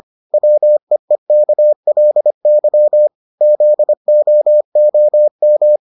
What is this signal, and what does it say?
Morse code, heard as a keyed tone, about 25 wpm, 600 Hz. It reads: WEEKLY ZOOM